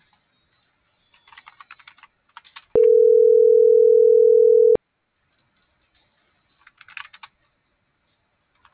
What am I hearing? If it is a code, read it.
Ringback tone (North America), 440 + 480 Hz.